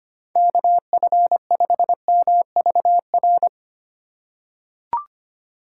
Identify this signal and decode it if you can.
Morse code, heard as a keyed tone, about 25 wpm, 700 Hz. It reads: KF5MVR
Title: USA Call Signs - 3x Character Spacing - 25wpm